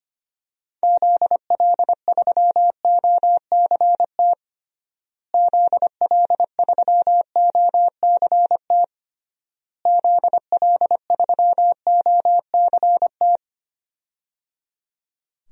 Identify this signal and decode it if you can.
Morse code, heard as a keyed tone, about 25 wpm, 700 Hz. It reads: ZL3OCT ZL3OCT ZL3OCT
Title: DXCC-100 Call Signs - 6 Characters - Repeated 3x - 25wpm